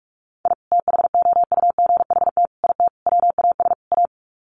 Morse code, heard as a keyed tone, about 44 wpm, 700 Hz. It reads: I THOUGHT IT WAS A
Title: Sets of 5 Words 45wpm